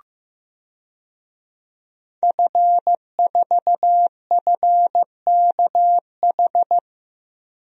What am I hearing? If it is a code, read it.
Morse code, heard as a keyed tone, about 15 wpm, 700 Hz. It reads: F4FKH
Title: DXCC-100 Call Signs - 5 Characters - Repeated 3x - 15wpm